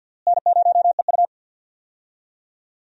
Morse code, heard as a keyed tone, about 53 wpm, 700 Hz. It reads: N0EU
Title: USA Call Signs - 4 Characters - Repeated 3x - 50wpm